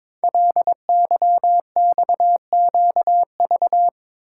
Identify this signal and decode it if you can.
Morse code, heard as a keyed tone, about 22 wpm, 700 Hz. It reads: LYXQV
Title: Code Groups - 5 Letters - 22wpm